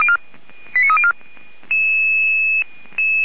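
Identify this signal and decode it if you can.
Five-tone selective call (ZVEI), heard as quick runs of five tones, tones of about 70 ms. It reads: 89304
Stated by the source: ZVEI five-tone selective call received by SDR, from the Feuerwehr Orlamünde fire brigade audio archive.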